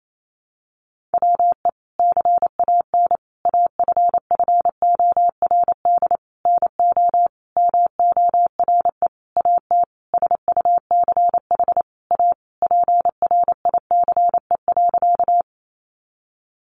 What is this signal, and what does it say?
Morse code, heard as a keyed tone, about 28 wpm, 700 Hz. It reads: WE CAN AFFORD NO MORE AT SUCH A PRICE.
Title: Sentences from Top 1100 Words 28wpm